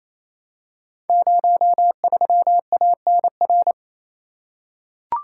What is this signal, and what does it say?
Morse code, heard as a keyed tone, about 27 wpm, 700 Hz. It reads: 03ANR
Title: Code Groups - 5 Letter:Numbers - 28wpm